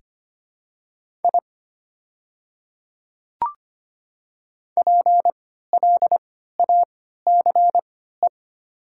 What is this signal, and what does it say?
Morse code, heard as a keyed tone, about 25 wpm, spaced as for 14 wpm, 690 Hz. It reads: I PLACE
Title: 3x Character Spacing - Top 200 Words - 25wpm